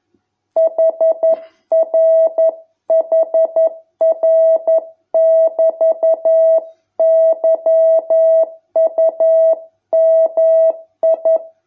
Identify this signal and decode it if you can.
Morse code, heard as a keyed tone, about 11 wpm, 650 Hz. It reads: HRHR<BT>YUMI